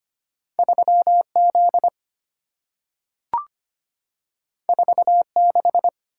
Morse code, heard as a keyed tone, about 25 wpm, 700 Hz. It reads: 3Z 46
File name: Two Letter-Number Combinations - 25wpm